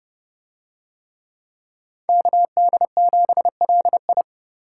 Morse code, heard as a keyed tone, about 29 wpm, 700 Hz. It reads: KD7LI